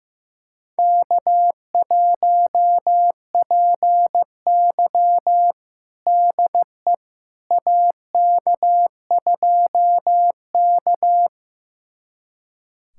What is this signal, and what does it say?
Morse code, heard as a keyed tone, about 16 wpm, 700 Hz. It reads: K1PY DE AK2K
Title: USA Callsign DE Callsign - 4 Characters - 15wpm